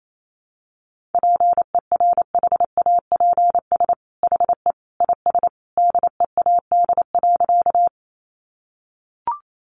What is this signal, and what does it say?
Morse code, heard as a keyed tone, about 28 wpm, 700 Hz. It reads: PERHAPS HE IS DEAD.
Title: Sentences from Top 1500 Words 28wpm